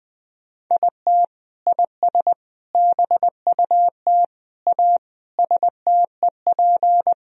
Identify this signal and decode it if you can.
Morse code, heard as a keyed tone, about 20 wpm, 700 Hz. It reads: IT IS BUT A STEP
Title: Sentences from Top 1000 Words - Review Part 2 - 20wpm